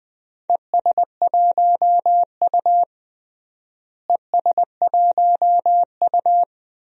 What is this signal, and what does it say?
Morse code, heard as a keyed tone, about 20 wpm, 700 Hz. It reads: ES1U ES1U
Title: DXCC-100 Call Signs - 4 Characters - Repeated 2x - 20wpm